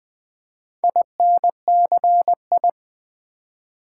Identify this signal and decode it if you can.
Morse code, heard as a keyed tone, about 20 wpm, 700 Hz. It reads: INCI
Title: Build A Word - 2 Letters At A Time - 8-Letter Words - 20wpm